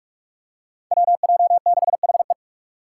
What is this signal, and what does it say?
Morse code, heard as a keyed tone, about 43 wpm, 700 Hz. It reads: WJ6HE
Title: Five Challenging Letter-Number Combinations 45wpm